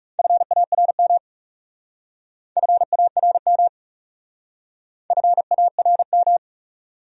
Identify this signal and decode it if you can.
Morse code, heard as a keyed tone, about 38 wpm, 700 Hz. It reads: FARM FARM FARM